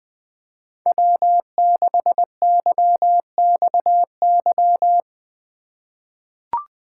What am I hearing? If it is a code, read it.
Morse code, heard as a keyed tone, about 20 wpm, 700 Hz. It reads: W6YXY
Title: USA Call Signs - 5 Characters - 2x Character Spacing - 20wpm